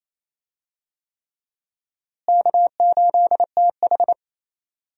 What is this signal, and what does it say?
Morse code, heard as a keyed tone, about 28 wpm, 700 Hz. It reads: K8TH
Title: USA Call Signs - 4 Characters - Repeated 3x - 28wpm